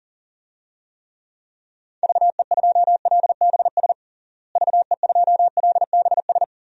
Morse code, heard as a keyed tone, about 40 wpm, 700 Hz. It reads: VE2LBS VE2LBS
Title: DXCC-100 Call Signs - 6 Characters - Repeated 2x - 40wpm